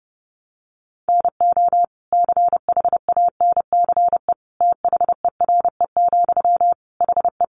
Morse code, heard as a keyed tone, about 30 wpm, 700 Hz. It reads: NO CHANCE THERE, HE
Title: Sentences from Top 800 Words 30wpm